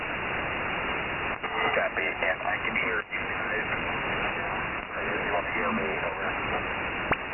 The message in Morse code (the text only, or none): none